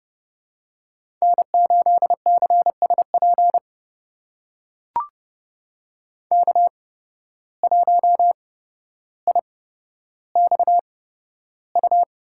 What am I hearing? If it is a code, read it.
Morse code, heard as a keyed tone, about 29 wpm, spaced as for 13 wpm, 700 Hz. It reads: N8CSP K1IXU
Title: USA Call Signs - 5 Characters - 8x Character Spacing - 30wpm